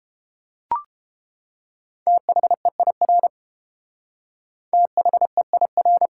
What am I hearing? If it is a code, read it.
Morse code, heard as a keyed tone, about 32 wpm, 700 Hz. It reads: THEIR THEIR